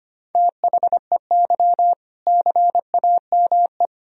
Morse code, heard as a keyed tone, about 25 wpm, 700 Hz. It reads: THEY CAME
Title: Sentences from Top 1000 Words - Review Part 3 - 25wpm